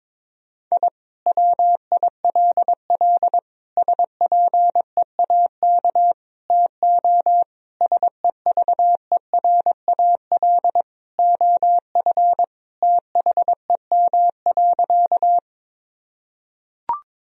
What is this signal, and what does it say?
Morse code, heard as a keyed tone, about 22 wpm, 700 Hz. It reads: I WILL SPEAK TO SEVERAL OF THEM.
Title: Sentences from Top 1000 Words - Review Part 2 - 22wpm